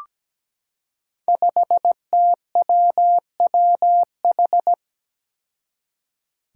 Morse code, heard as a keyed tone, about 17 wpm, 700 Hz. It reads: 5TWWH
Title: Five Challenging Letter-Number Combinations 17wpm